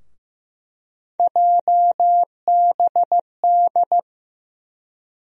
Morse code, heard as a keyed tone, about 15 wpm, 700 Hz. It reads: JBD
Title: USA Call Sign Postfixes - 3 Letters 15wpm